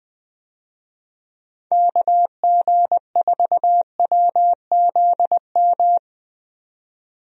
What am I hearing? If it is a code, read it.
Morse code, heard as a keyed tone, about 20 wpm, 700 Hz. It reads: KG4WZM
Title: USA Call Signs - 6 Characters - 20wpm